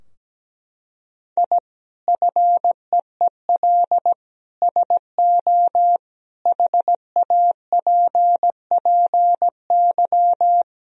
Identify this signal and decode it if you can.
Morse code, heard as a keyed tone, about 17 wpm, 700 Hz. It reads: I FEEL SO HAPPY
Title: Sentences from Top 1000 Words - Review Part 3 - 17wpm